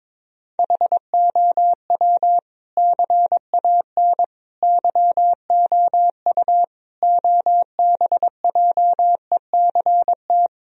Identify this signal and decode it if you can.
Morse code, heard as a keyed tone, about 21 wpm, 700 Hz. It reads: HOW CAN YOU OBJECT
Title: Sentences from Top 1000 Words - Review Part 3 - 22wpm